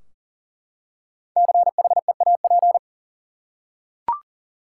Morse code, heard as a keyed tone, about 40 wpm, 700 Hz. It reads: CHEAP
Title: Five Letter Words - 40wpm